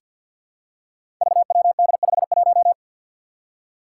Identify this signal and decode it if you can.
Morse code, heard as a keyed tone, about 49 wpm, 700 Hz. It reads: VWD51
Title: Five Challenging Letter-Number Combinations 50wpm